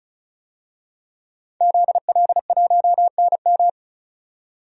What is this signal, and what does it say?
Morse code, heard as a keyed tone, about 36 wpm, 700 Hz. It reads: ZL1NM
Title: DXCC-100 Call Signs - 5 Characters - 35wpm